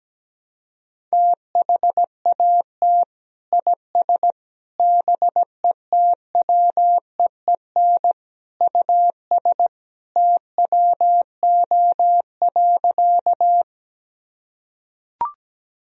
Morse code, heard as a keyed tone, about 17 wpm, 700 Hz. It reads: THAT IS BETWEEN US TWO.
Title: Sentences from Top 1000 Words - Review Part 1 - 17wpm